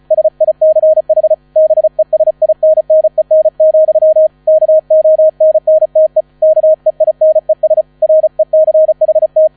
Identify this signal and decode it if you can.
Morse code, heard as a keyed tone, about 36 wpm, 630 Hz. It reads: SICH BESINNEN, KONNTE KEINES RECHT
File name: 35wpm